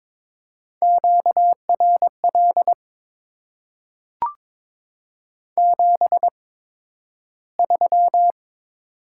Morse code, heard as a keyed tone, about 22 wpm, spaced as for 12 wpm, 700 Hz. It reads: QRL 73
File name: Top 20 QSO Elements - 8x Character Spacing - 22wpm